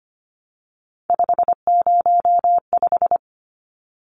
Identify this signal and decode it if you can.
Morse code, heard as a keyed tone, about 25 wpm, 700 Hz. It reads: 505
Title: Numbers - 3-Digit 25wpm